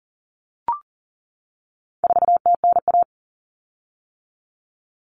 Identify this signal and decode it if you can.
Morse code, heard as a keyed tone, about 40 wpm, 700 Hz. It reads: 4TNA
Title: Common Cut Numbers - 4-Digit 40wpm